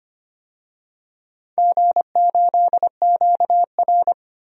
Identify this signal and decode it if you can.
Morse code, heard as a keyed tone, about 25 wpm, 700 Hz. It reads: G8QR
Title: DXCC-100 Call Signs - 4 Characters - 25wpm